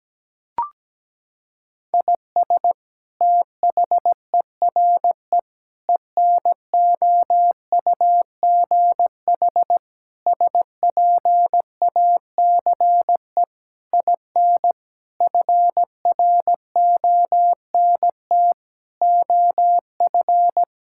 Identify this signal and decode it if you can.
Morse code, heard as a keyed tone, about 17 wpm, 700 Hz. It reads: IS THERE ENOUGH SPACE IN FRONT OF
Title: Sentences from Top 1000 Words - Review Part 5 - 17wpm